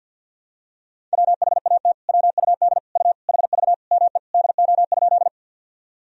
Morse code, heard as a keyed tone, about 50 wpm, 700 Hz. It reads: WHAT WUD U HV ME DO?
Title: Sentences from Top 200 Words and Abbreviations 50wpm